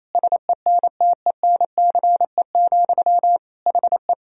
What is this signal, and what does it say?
Morse code, heard as a keyed tone, about 28 wpm, 700 Hz. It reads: SENTENCE, HE
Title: Sentences from Top 1000 Words - Review Part 1 - 28wpm